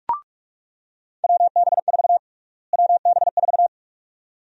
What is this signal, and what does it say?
Morse code, heard as a keyed tone, about 42 wpm, 700 Hz. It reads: WB4 WB4
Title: USA Call Sign Prefixes - Repeated 2x - 45wpm